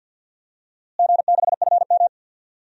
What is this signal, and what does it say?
Morse code, heard as a keyed tone, about 52 wpm, 700 Hz. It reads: G6FM
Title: DXCC-100 Call Signs - 4 Characters - Repeated 3x - 50wpm